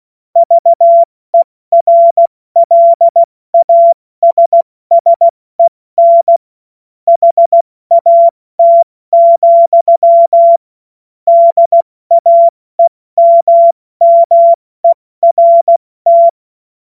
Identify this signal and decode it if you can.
Morse code, heard as a keyed tone, about 16 wpm, spaced as for 14 wpm, 680 Hz. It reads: VERLASSEN HAT, DAEMMERT
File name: Morsekurs / Witze / Witz 30 Januar 2024 70 F80